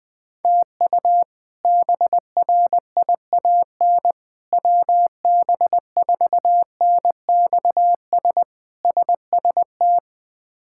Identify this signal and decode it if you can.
Morse code, heard as a keyed tone, about 20 wpm, 700 Hz. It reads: TU BRIAN WB4NXS SST
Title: K1USN SST QSO Practice - 20wpm